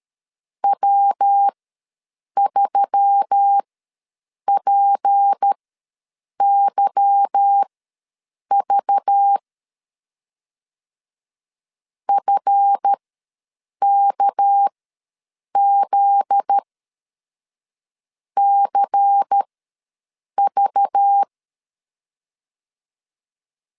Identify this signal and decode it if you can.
Morse code, heard as a keyed tone, about 13 wpm, spaced as for 6 wpm, 800 Hz. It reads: W3PYV FKZCV
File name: LEZIONI / 8 YJ45 / 8 Y ws 12wpm 3409